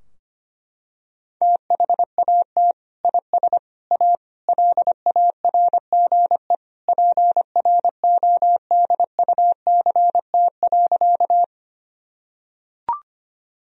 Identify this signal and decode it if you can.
Morse code, heard as a keyed tone, about 25 wpm, 700 Hz. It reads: THAT IS A LARGE PRODUCT.
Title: Sentences from Top 1000 Words - Review Part 2 - 25wpm